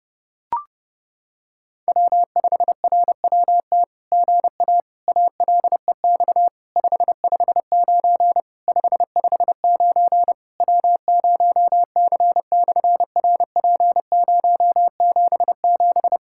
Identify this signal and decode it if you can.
Morse code, heard as a keyed tone, about 30 wpm, 700 Hz. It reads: W5RWT GA ALEX 559 559 W0C/RP077
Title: SOTA QSO Practice 30wpm